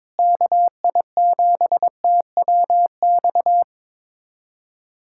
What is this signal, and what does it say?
Morse code, heard as a keyed tone, about 22 wpm, 700 Hz. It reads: KI7TWX
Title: USA Callsign DE Callsign - 6 Characters - 22wpm